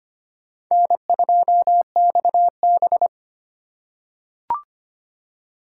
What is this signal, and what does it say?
Morse code, heard as a keyed tone, about 25 wpm, 700 Hz. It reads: N2XB
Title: USA Call Signs - 3x Character Spacing - 25wpm